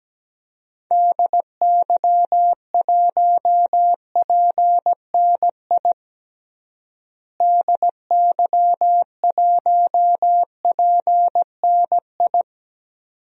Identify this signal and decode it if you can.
Morse code, heard as a keyed tone, about 17 wpm, 700 Hz. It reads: DY1PNI DY1PNI
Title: DXCC-100 Call Signs - 6 Characters - Repeated 2x - 17wpm